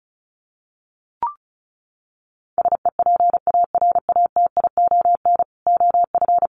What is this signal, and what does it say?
Morse code, heard as a keyed tone, about 34 wpm, 700 Hz. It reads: SEPARATION OF
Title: Sets of 5 Words 35wpm